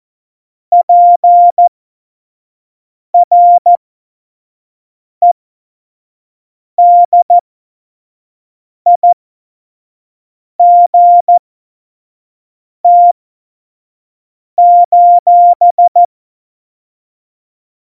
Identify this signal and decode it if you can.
Morse code, heard as a keyed tone, about 14 wpm, spaced as for 5 wpm, 700 Hz. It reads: PREDIGT:
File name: Morsekurs / Witze / Witz 08. Dezember2020, 25 F70